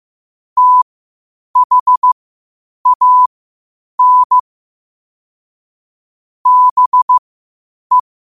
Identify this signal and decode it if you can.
Morse code, heard as a keyed tone, about 15 wpm, spaced as for 8 wpm, 1000 Hz. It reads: THAN BE
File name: qotdfiles / 8wpm